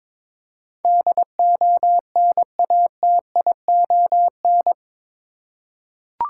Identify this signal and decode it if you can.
Morse code, heard as a keyed tone, about 21 wpm, 700 Hz. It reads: DONATION
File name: practice / 22wpm / Eight Letter Words - 22wpm